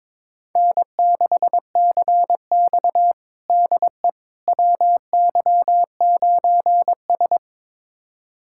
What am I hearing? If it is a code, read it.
Morse code, heard as a keyed tone, about 22 wpm, 700 Hz. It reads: N6CX DE WY9S
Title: USA Callsign DE Callsign - 4 Characters - 22wpm